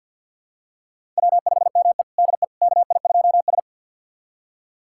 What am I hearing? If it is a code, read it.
Morse code, heard as a keyed tone, about 52 wpm, 700 Hz. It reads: W5ME DE KI2S
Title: USA Callsign DE Callsign - 4 Characters - 50wpm